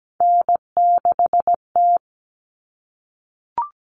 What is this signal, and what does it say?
Morse code, heard as a keyed tone, about 17 wpm, 700 Hz. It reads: N6T
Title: All Cut Numbers - 3-Digit 17wpm